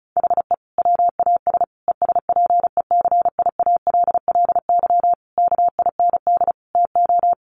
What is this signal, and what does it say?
Morse code, heard as a keyed tone, about 34 wpm, 700 Hz. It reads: HE WAS ESPECIALLY KIND TO
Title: Sentences from Top 700 Words 35wpm